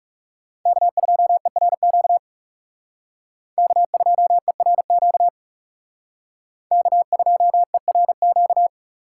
Morse code, heard as a keyed tone, about 36 wpm, 700 Hz. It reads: K2ERQ K2ERQ K2ERQ
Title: Speed Racing - USA Call Signs 30wpm